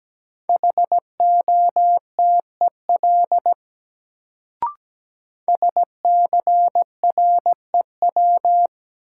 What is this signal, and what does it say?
Morse code, heard as a keyed tone, about 17 wpm, 700 Hz. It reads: HOTEL SCREW
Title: Sentences from Top 1600 Words - New Word Drill Alt - 17wpm